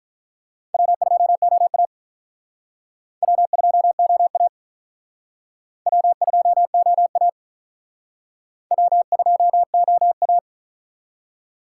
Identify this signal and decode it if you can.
Morse code, heard as a keyed tone, about 42 wpm, 700 Hz. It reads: W2OA W2OA W2OA W2OA
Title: Speed Racing - USA 1x2 Call Signs 35wpm